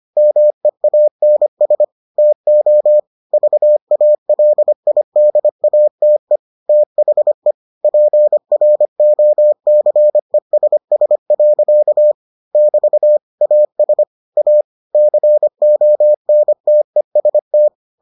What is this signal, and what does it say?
Morse code, heard as a keyed tone, about 25 wpm, 600 Hz. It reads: MEANS TO VALIDATE THE PROCESS. <BT> AS A CONTEST